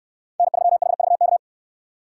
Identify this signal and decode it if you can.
Morse code, heard as a keyed tone, about 69 wpm, 700 Hz. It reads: N3HVK